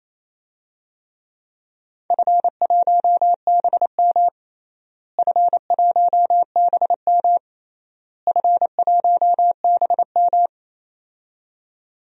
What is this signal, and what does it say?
Morse code, heard as a keyed tone, about 28 wpm, 700 Hz. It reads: F1BM F1BM F1BM
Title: DXCC-100 Call Signs - 4 Characters - Repeated 3x - 28wpm